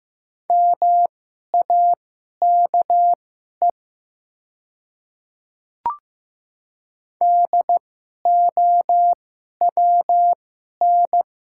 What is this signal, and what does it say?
Morse code, heard as a keyed tone, about 15 wpm, spaced as for 11 wpm, 700 Hz. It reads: MAKE DOWN
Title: 2x Character Spacing - Top 100 Words - 15wpm